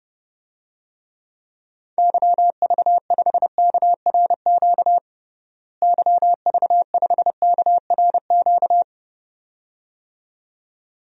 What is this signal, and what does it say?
Morse code, heard as a keyed tone, about 32 wpm, 700 Hz. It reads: YV5KRQ YV5KRQ
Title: DXCC-100 Call Signs - 6 Characters - Repeated 2x - 30wpm